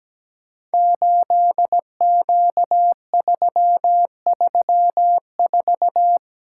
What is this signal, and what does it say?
Morse code, heard as a keyed tone, about 17 wpm, 700 Hz. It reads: 8Q334
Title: Code Groups - 5 Letter:Numbers - 17wpm